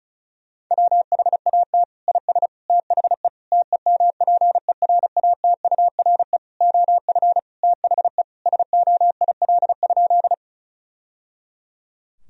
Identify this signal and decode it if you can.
Morse code, heard as a keyed tone, about 34 wpm, 700 Hz. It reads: WHAT IS THE TEMPERATURE OF THE SOIL?
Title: Sentences from Top 1000 Words - Review Part 4 - 35wpm